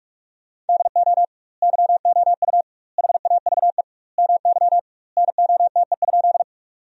Morse code, heard as a keyed tone, about 46 wpm, 700 Hz. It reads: DO YOU HAVE MY NOTE?
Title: 2x Word-Spacing - Sentences from Top 500 Words - 45wpm